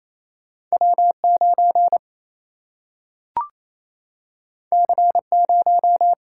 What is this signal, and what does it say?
Morse code, heard as a keyed tone, about 28 wpm, 700 Hz. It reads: W9 C0
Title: Single-Letter Single-Number 28wpm